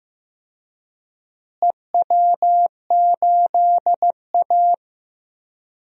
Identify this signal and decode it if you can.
Morse code, heard as a keyed tone, about 15 wpm, 700 Hz. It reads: EW8A
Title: DXCC-100 Call Signs - 4 Characters - Repeated 2x - 15wpm